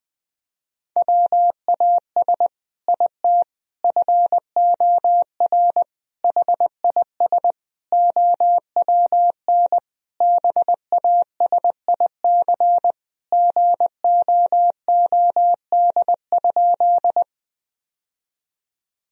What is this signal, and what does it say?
Morse code, heard as a keyed tone, about 20 wpm, 700 Hz. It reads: WAS IT FOR HIS OWN BASIC GOOD?
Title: Sentences from Top 1000 Words - Review Part 3 - 20wpm